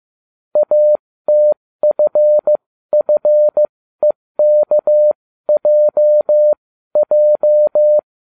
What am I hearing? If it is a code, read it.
Morse code, heard as a keyed tone, about 15 wpm, spaced as for 13 wpm, 600 Hz. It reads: ATFFEKJJ